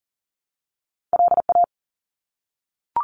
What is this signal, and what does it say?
Morse code, heard as a keyed tone, about 42 wpm, 700 Hz. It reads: LA
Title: ARRL Sections 40wpm